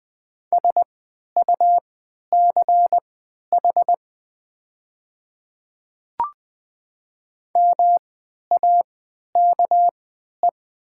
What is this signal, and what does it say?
Morse code, heard as a keyed tone, about 20 wpm, spaced as for 11 wpm, 700 Hz. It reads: SUCH MAKE
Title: 3x Character Spacing - Top 500 Words - 20wpm